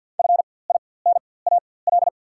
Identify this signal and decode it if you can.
Morse code, heard as a keyed tone, about 53 wpm, spaced as for 24 wpm, 690 Hz. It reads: FINAL
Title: 4x Character Spacing - Top 500 Words - 50wpm